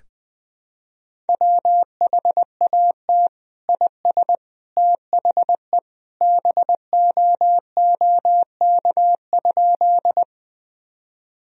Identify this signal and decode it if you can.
Morse code, heard as a keyed tone, about 20 wpm, 700 Hz. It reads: WHAT IS THE BOOK?
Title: Sentences from Top 1000 Words - Review Part 1 - 20wpm